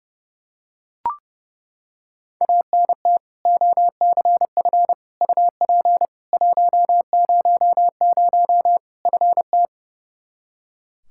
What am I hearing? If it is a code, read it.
Morse code, heard as a keyed tone, about 32 wpm, 700 Hz. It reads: ANT OCF UP 100 FT
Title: K1USN SST QSO Practice 30wpm